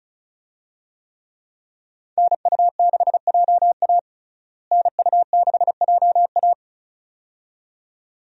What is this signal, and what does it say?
Morse code, heard as a keyed tone, about 35 wpm, 700 Hz. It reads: NU6JA NU6JA
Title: USA Call Signs - 5 Characters - Repeated 2x - 35wpm